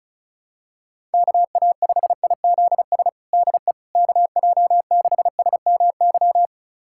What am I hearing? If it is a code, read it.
Morse code, heard as a keyed tone, about 36 wpm, 700 Hz. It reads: KA5IZS DE KJ6SMY
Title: USA Callsign DE Callsign - 6 Characters - 35wpm